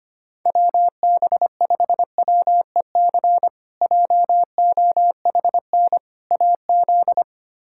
Morse code, heard as a keyed tone, about 25 wpm, 700 Hz. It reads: WB5WEC JOHN AZ
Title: K1USN SST QSO Practice - 25wpm